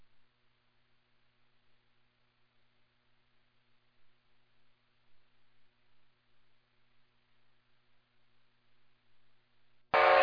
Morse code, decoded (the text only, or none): none